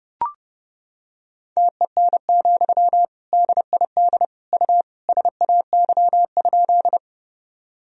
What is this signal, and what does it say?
Morse code, heard as a keyed tone, about 30 wpm, 700 Hz. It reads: TEN, DID U SAY?
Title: Sentences from Top 400 Words and Abbreviations 30wpm